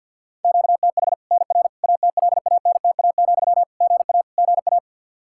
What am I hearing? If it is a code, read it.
Morse code, heard as a keyed tone, about 50 wpm, 700 Hz. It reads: QTH NR ATLANTA, GA GA
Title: Practice QSO - Exchange 1 - Beginning - 50wpm